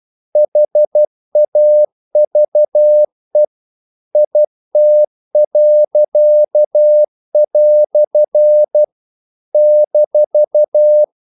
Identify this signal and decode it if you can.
Morse code, heard as a keyed tone, about 12 wpm, 600 Hz. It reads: HAVE IT." -
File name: CW / Practice / EnglishQuotations0112